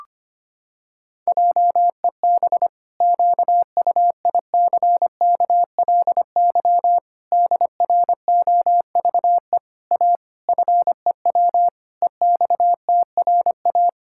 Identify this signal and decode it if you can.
Morse code, heard as a keyed tone, about 25 wpm, 700 Hz. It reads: JEB QUICKLY DROVE A FEW EXTRA
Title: Pangram Challenge v2 - 25wpm